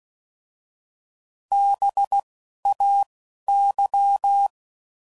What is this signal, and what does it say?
Morse code, heard as a keyed tone, about 16 wpm, 800 Hz. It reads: B A Y